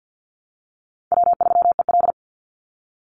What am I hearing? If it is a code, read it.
Morse code, heard as a keyed tone, about 50 wpm, 700 Hz. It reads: W3EL
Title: Call Signs Repeated 2x - 50wpm